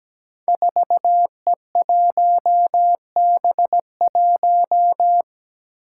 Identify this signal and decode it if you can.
Morse code, heard as a keyed tone, about 17 wpm, 700 Hz. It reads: 4E1B1
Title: Five Challenging Letter-Number Combinations 17wpm